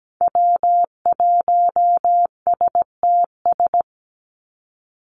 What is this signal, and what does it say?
Morse code, heard as a keyed tone, about 17 wpm, 700 Hz. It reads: W1STS
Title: USA Call Signs 17wpm